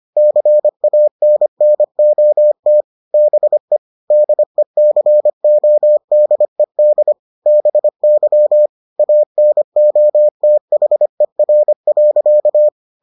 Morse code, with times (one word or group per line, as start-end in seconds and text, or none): 0.2-2.8 CANNOT
3.1-3.8 BE
4.1-7.1 DECODED
7.5-8.7 BY
9.0-12.7 ANOTHER.